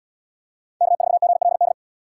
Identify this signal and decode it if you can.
Morse code, heard as a keyed tone, about 68 wpm, 700 Hz. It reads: K4GLD